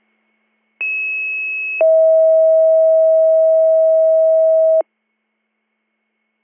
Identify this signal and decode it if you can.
Two-tone page, heard as a one-second tone, then a three-second tone, about 2600 then 640 Hz. It is heard from 0.8 to 4.8 s.